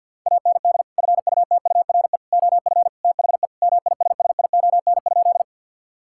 Morse code, heard as a keyed tone, about 50 wpm, 700 Hz. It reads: AND FUTURE OF THE MISSION?
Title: Sentences from Top 1600 Words 50wpm